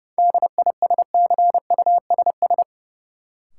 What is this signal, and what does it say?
Morse code, heard as a keyed tone, about 32 wpm, 700 Hz. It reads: DISCUSS
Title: Sentences from Top 1000 Words - New Word Drill - 30wpm